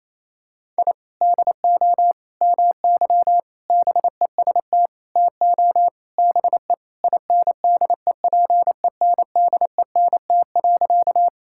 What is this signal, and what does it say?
Morse code, heard as a keyed tone, about 28 wpm, 700 Hz. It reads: I DO MY BEST TO BE INDEPENDENT.
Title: Sentences from Top 1000 Words - Review Part 6 - 28wpm